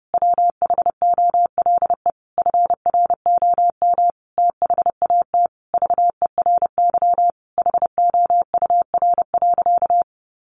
Morse code, heard as a keyed tone, about 30 wpm, 700 Hz. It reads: WHOLE FROM THAT VERY HOUR.
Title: Sentences from Top 1500 Words 30wpm